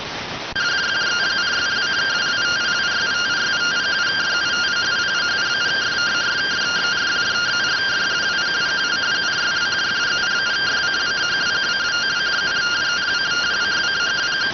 Two-tone teleprinter signal, about 170 Hz shift.